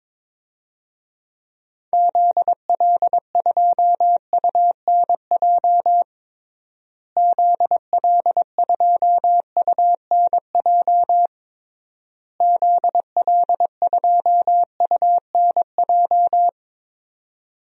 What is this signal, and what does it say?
Morse code, heard as a keyed tone, about 22 wpm, 700 Hz. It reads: ZL2UNJ ZL2UNJ ZL2UNJ
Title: DXCC-100 Call Signs - 6 Characters - Repeated 3x - 22wpm